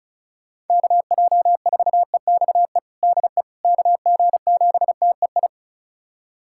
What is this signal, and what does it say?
Morse code, heard as a keyed tone, about 34 wpm, 690 Hz. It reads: KJ4EXE DE KG7TEI
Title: USA Callsign DE Callsign - 6 Characters - 35wpm